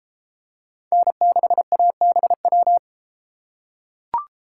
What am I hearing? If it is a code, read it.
Morse code, heard as a keyed tone, about 32 wpm, 700 Hz. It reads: N6ABW